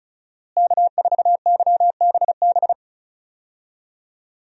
Morse code, heard as a keyed tone, about 36 wpm, 700 Hz. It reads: K4YBB
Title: USA Call Signs - 5 Characters - Repeated 2x - 35wpm